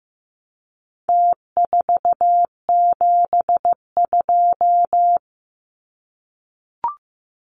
Morse code, heard as a keyed tone, about 15 wpm, 700 Hz. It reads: T472
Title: Numbers - Contest Style - 4-Digit 15wpm